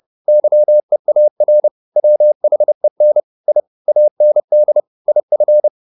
Morse code, heard as a keyed tone, about 30 wpm, 600 Hz. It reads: YEAR WHEN I AND IF